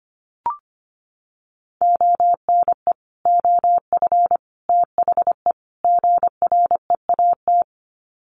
Morse code, heard as a keyed tone, about 25 wpm, 700 Hz. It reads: ONE OF THE GREAT
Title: Sets of 4 Words 25wpm